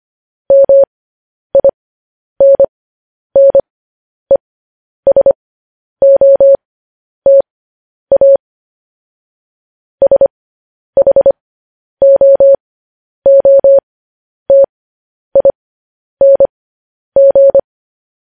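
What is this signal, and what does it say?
Morse code, heard as a keyed tone, about 25 wpm, spaced as for 10 wpm, 550 Hz. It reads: MINNESOTA SHOOTING
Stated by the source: Morse code news headlines sent at 10 wpm for practice.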